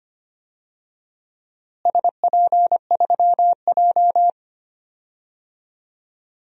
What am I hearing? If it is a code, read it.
Morse code, heard as a keyed tone, about 25 wpm, 700 Hz. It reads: SP3J
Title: DXCC-100 Call Signs - 4 Characters - Repeated 3x - 25wpm